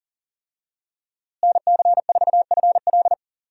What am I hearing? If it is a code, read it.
Morse code, heard as a keyed tone, about 40 wpm, 700 Hz. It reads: NC4FL
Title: USA Call Signs - 5 Characters - Repeated 2x - 40wpm